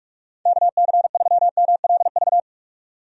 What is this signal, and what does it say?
Morse code, heard as a keyed tone, about 46 wpm, 700 Hz. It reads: KC3MLV
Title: USA Call Signs - 6 Characters - 45wpm